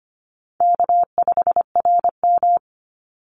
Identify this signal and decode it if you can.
Morse code, heard as a keyed tone, about 25 wpm, 700 Hz. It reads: K5RM
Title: Easy Call Signs - 25wpm